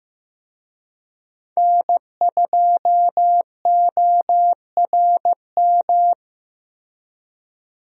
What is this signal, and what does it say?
Morse code, heard as a keyed tone, about 15 wpm, 700 Hz. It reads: N2ORM
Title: USA Call Signs - 5 Characters - Repeated 3x - 15wpm